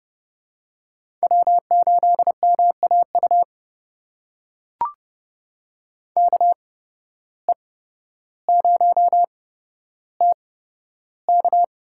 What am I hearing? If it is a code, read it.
Morse code, heard as a keyed tone, about 30 wpm, spaced as for 13 wpm, 700 Hz. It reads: W8MAU KE0TK
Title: USA Call Signs - 5 Characters - 8x Character Spacing - 30wpm